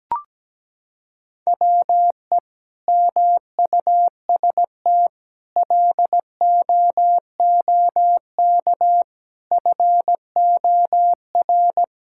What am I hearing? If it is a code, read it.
Morse code, heard as a keyed tone, about 17 wpm, 700 Hz. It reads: WE MUST LOOK FOR
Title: Sentences from Top 1600 Words 17wpm